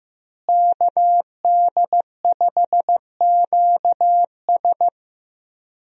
Morse code, heard as a keyed tone, about 15 wpm, 700 Hz. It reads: KD5QS
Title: USA Call Signs - 5 Characters - Repeated 2x - 15wpm